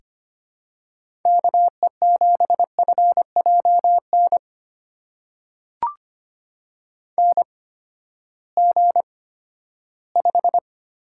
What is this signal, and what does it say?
Morse code, heard as a keyed tone, about 25 wpm, spaced as for 14 wpm, 700 Hz. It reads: KE7FJN NG5
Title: USA Call Signs - 8x Character Spacing - 25wpm